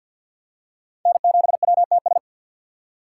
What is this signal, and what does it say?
Morse code, heard as a keyed tone, about 51 wpm, 700 Hz. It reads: N7WTS
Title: USA Call Signs - 5 Characters - 2x Character Spacing - 50wpm